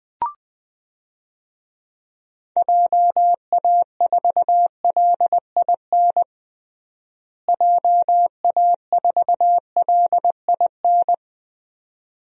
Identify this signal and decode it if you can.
Morse code, heard as a keyed tone, about 20 wpm, 700 Hz. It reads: JA4LIN JA4LIN
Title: DXCC-100 Call Signs - 6 Characters - Repeated 2x - 20wpm